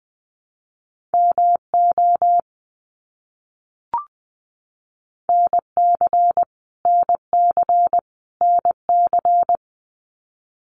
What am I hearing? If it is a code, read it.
Morse code, heard as a keyed tone, about 20 wpm, 700 Hz. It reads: MO NC NC NC
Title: US States Repeated 3x 20wpm